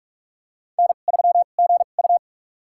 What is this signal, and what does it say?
Morse code, heard as a keyed tone, about 42 wpm, spaced as for 31 wpm, 700 Hz. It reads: N3GU
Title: USA Call Signs - 4 Characters - 2x Character Spacing - 45wpm